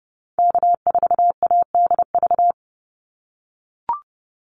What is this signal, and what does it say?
Morse code, heard as a keyed tone, about 30 wpm, 700 Hz. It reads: K4ADV
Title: NA SOTA Activator Callsigns 30wpm